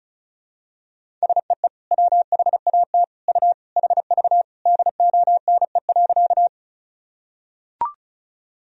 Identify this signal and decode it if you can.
Morse code, heard as a keyed tone, about 34 wpm, 700 Hz. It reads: SEE WHAT U HV DONE.
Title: Sentences from Top 500 Words and Abbreviations 35wpm